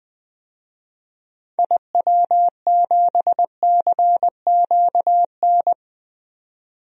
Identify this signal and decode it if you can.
Morse code, heard as a keyed tone, about 20 wpm, 700 Hz. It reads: IW7CQN
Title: DXCC-100 Call Signs - 6 Characters - 20wpm